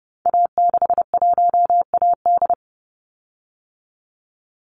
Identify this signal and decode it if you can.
Morse code, heard as a keyed tone, about 30 wpm, 700 Hz. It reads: A61AD
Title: Call Signs Repeated 2x - 30wpm